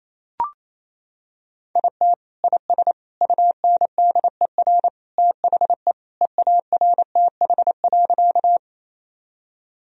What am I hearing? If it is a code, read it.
Morse code, heard as a keyed tone, about 28 wpm, 700 Hz. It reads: IT IS UNDER THE EARTH.
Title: Sentences from Top 1000 Words - Review Part 1 - 28wpm